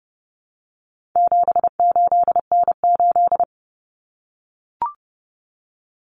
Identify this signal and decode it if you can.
Morse code, heard as a keyed tone, about 30 wpm, 700 Hz. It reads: 78N8
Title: Common Cut Numbers - 4-Digit 30wpm